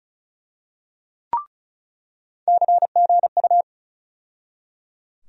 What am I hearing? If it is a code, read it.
Morse code, heard as a keyed tone, about 36 wpm, 700 Hz. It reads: CGU
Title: USA Call Sign Postfixes - 3 Letters 35wpm